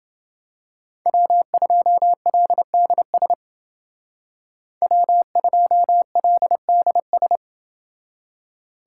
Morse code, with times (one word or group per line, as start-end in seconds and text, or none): 1.1-3.3 W2LDS
4.8-7.4 W2LDS